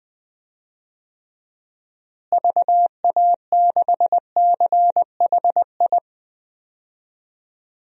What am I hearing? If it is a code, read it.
Morse code, heard as a keyed tone, about 20 wpm, 700 Hz. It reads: VA6CHI
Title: DXCC-100 Call Signs - 6 Characters - Repeated 2x - 20wpm